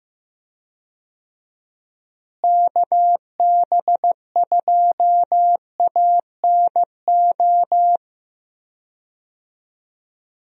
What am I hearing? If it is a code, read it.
Morse code, heard as a keyed tone, about 15 wpm, 700 Hz. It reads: KB2ANO
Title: USA Call Signs - 6 Characters - Repeated 3x - 15wpm